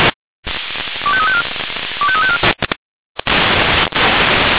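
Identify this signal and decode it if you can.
Five-tone selective call (ZVEI), heard as quick runs of five tones, tones of about 70 ms. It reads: 25354 25354
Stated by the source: ZVEI five-tone call received by SDR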